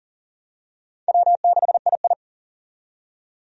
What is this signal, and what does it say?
Morse code, heard as a keyed tone, about 38 wpm, 700 Hz. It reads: W6II
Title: USA Call Signs - 4 Characters - 40wpm